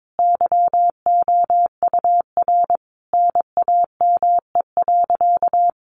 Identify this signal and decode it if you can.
Morse code, heard as a keyed tone, about 22 wpm, 700 Hz. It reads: YOUR NAME.
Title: Sentences from Top 200 Words 22wpm